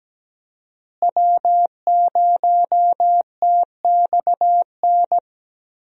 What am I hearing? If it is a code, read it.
Morse code, heard as a keyed tone, about 17 wpm, 700 Hz. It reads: W0TXN
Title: USA Call Signs - 5 Characters - Repeated 3x - 17wpm